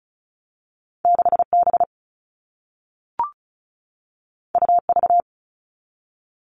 Morse code, heard as a keyed tone, about 36 wpm, 700 Hz. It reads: 6B UV
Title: All Cut Numbers - 2-Digit 35wpm